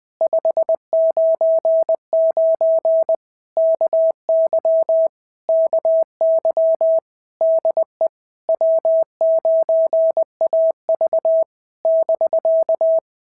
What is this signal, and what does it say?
Morse code, heard as a keyed tone, about 20 wpm, 650 Hz. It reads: 599 KY KY DE W9AV <BK>
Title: POTA QSO Practice - 20wpm